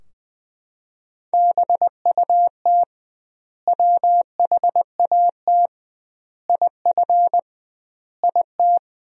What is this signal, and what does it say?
Morse code, heard as a keyed tone, about 21 wpm, 700 Hz. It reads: BUT WHAT IF IT
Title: 2x Word-Spacing - Sentences from Top 100 Words - 20wpm